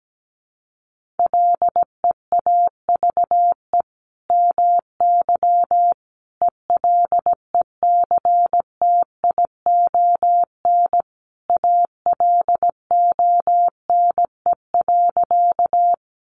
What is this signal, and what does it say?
Morse code, heard as a keyed tone, about 17 wpm, 700 Hz. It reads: LEAVE MY ELECTION ALONE.
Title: Sentences from Top 1500 Words 17wpm